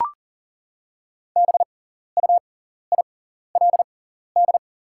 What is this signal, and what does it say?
Morse code, heard as a keyed tone, about 40 wpm, spaced as for 14 wpm, 700 Hz. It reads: BUILD
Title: 6x Character Spacing - Top 300 Words - 40wpm